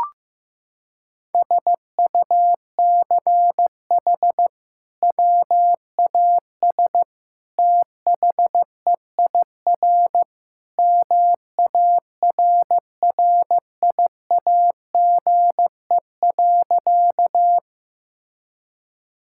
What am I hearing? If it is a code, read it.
Morse code, heard as a keyed tone, about 15 wpm, 700 Hz. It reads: SUCH WAS THEIR MARRIAGE.
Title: Sentences from Top 1000 Words - Review Part 5 - 15wpm